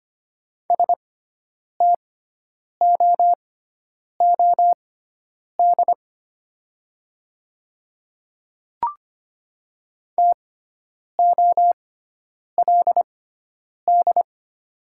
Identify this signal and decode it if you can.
Morse code, heard as a keyed tone, about 25 wpm, spaced as for 9 wpm, 700 Hz. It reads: STOOD TOLD
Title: 6x Character Spacing - Top 500 Words - 25wpm